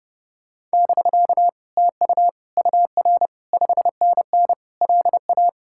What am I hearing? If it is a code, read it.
Morse code, heard as a keyed tone, about 29 wpm, 700 Hz. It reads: <BK> TU UR 5NN LA